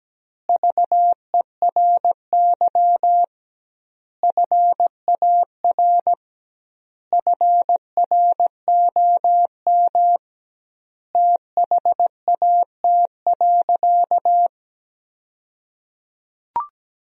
Morse code, heard as a keyed tone, about 17 wpm, 700 Hz. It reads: VERY FAR FROM THAT.
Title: 2x Word-Spacing - Sentences from Top 300 Words - 17wpm